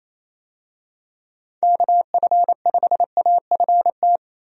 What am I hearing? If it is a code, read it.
Morse code, heard as a keyed tone, about 27 wpm, 700 Hz. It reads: KF5AFT
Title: USA Call Signs - 6 Characters - Repeated 3x - 28wpm